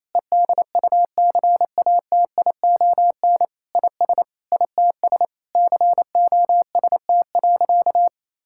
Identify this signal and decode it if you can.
Morse code, heard as a keyed tone, about 28 wpm, 700 Hz. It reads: EDUCATION IS ITS COST.
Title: Sentences from Top 1000 Words - Review Part 6 - 28wpm